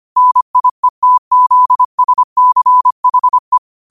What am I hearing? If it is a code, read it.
Morse code, heard as a keyed tone, about 25 wpm, 1000 Hz. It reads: NIETZSCHE